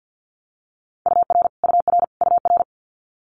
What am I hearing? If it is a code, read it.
Morse code, heard as a keyed tone, about 48 wpm, 700 Hz. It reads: UR UR UR
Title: Abbreviations Repeated 3x - 50wpm